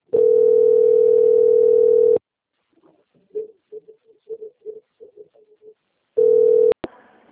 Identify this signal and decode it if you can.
Ringback tone (North America), 440 + 480 Hz.